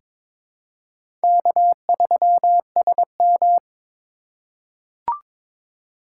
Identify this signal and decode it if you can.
Morse code, heard as a keyed tone, about 22 wpm, 700 Hz. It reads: K3SM
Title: USA Call Signs - 4 Characters - 4x Character Spacing - 22wpm